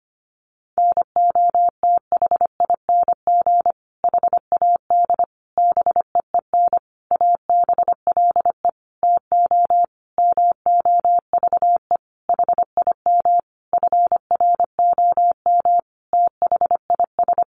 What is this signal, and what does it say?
Morse code, heard as a keyed tone, about 25 wpm, 700 Hz. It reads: NOTHING HAD BEEN ABLE TO MOVE HIM FROM THIS
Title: Sentences from Top 1300 Words 25wpm